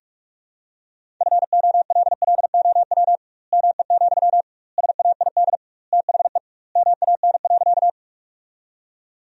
Morse code, heard as a keyed tone, about 41 wpm, 700 Hz. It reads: FOLLOW ME, SAID THE MAN.
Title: 2x Word-Spacing - Sentences from Top 200 Words - 45wpm